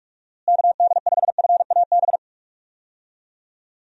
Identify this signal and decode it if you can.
Morse code, heard as a keyed tone, about 41 wpm, 700 Hz. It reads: KD5FAB
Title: USA Call Signs - 6 Characters - Repeated 2x - 45wpm